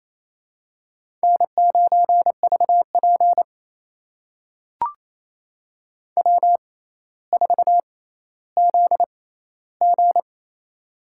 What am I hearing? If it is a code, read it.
Morse code, heard as a keyed tone, about 28 wpm, spaced as for 14 wpm, 700 Hz. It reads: N9VP W4ZG
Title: USA Call Signs - 4 Characters - 6x Character Spacing - 28wpm